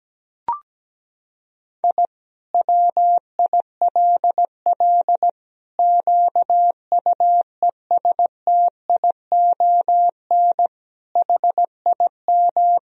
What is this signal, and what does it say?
Morse code, heard as a keyed tone, about 17 wpm, 700 Hz. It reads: I WILL QUESTION HIM
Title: Sentences from Top 1000 Words - Review Part 2 - 17wpm